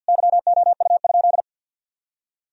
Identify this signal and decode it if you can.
Morse code, heard as a keyed tone, about 50 wpm, 700 Hz. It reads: YOU?
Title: Sentences from Top 1000 Words - Review Part 2 - 50wpm